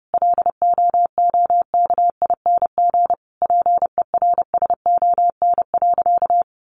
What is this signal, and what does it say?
Morse code, heard as a keyed tone, about 28 wpm, 700 Hz. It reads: LOOKING PERSON.
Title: Easy Phrases 30wpm